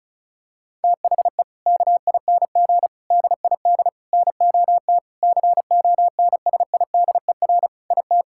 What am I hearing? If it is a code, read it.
Morse code, heard as a keyed tone, about 36 wpm, 700 Hz. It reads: THE KING DID NOT CONSIDER IT
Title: Sentences from Top 1000 Words - Review Part 5 - 35wpm